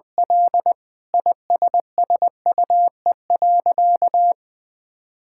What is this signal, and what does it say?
Morse code, heard as a keyed tone, about 21 wpm, 700 Hz. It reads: L ISSUE.
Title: Sentences from Top 1000 Words - Review Part 6 - 20wpm